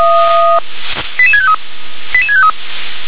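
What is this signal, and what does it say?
Five-tone selective call (ZVEI), heard as quick runs of five tones, tones of about 70 ms. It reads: 88653 88653
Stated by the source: ZVEI five-tone selective call 88861 received by SDR.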